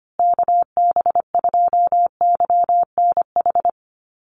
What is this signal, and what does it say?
Morse code, heard as a keyed tone, about 25 wpm, 700 Hz. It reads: KB2YNH
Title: USA Call Signs Repeated 3x 25wpm